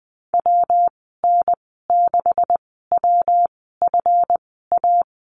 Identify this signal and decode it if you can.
Morse code, heard as a keyed tone, about 20 wpm, spaced as for 14 wpm, 700 Hz. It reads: WN6WFA
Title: 147 Lesson 24g - Letter y - Call Signs 20wpm